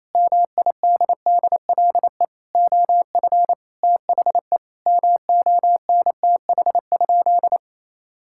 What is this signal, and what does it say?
Morse code, heard as a keyed tone, about 28 wpm, 700 Hz. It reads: MIDDLE OF THE MONTH?
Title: Sentences from Top 1000 Words - Review Part 6 - 28wpm